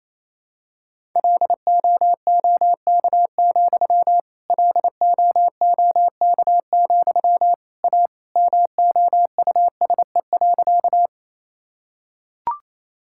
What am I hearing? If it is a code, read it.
Morse code, heard as a keyed tone, about 28 wpm, 700 Hz. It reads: LOOK, LOOK, A MOUSE.
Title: Sentences from Top 1000 Words - Review Part 5 - 28wpm